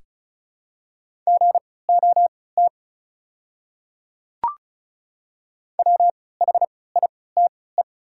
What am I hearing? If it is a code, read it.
Morse code, heard as a keyed tone, about 34 wpm, spaced as for 20 wpm, 700 Hz. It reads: GOT WHITE
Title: 3x Character Spacing - Top 300 Words - 35wpm